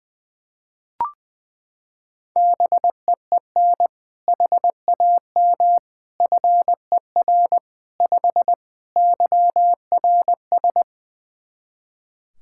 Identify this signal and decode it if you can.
Morse code, heard as a keyed tone, about 21 wpm, 700 Hz. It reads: BEEN HAM FER 5 YRS
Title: Practice QSO - Exchange 3 - More Info - 20wpm